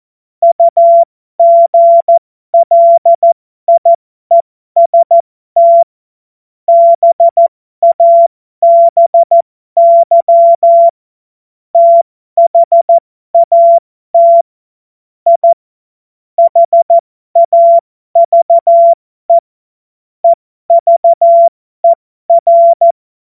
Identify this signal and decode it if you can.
Morse code, heard as a keyed tone, about 14 wpm, spaced as for 12 wpm, 680 Hz. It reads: UGLIEST BABY THAT I HAVE EVER
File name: Morsekurs / Witze